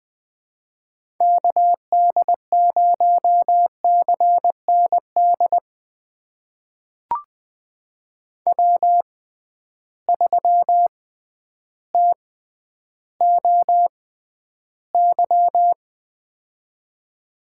Morse code, heard as a keyed tone, about 20 wpm, spaced as for 11 wpm, 700 Hz. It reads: KD0CND W3TOY
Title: USA Call Signs - 6x Character Spacing - 20wpm